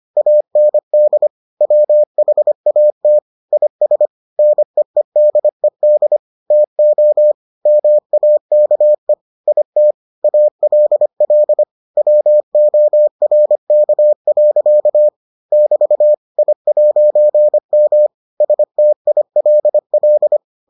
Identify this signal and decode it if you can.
Morse code, heard as a keyed tone, about 25 wpm, 600 Hz. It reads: AND WHAT IS NEEDED TO MAKE IT ALL WORK. <BT> I'M STILL